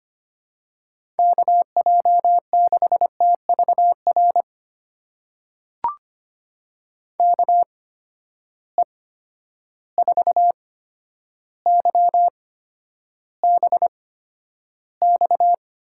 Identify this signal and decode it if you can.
Morse code, heard as a keyed tone, about 25 wpm, spaced as for 11 wpm, 700 Hz. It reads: KJ6TVR KE4YBX
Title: USA Call Signs - 6 Characters - 8x Character Spacing - 25wpm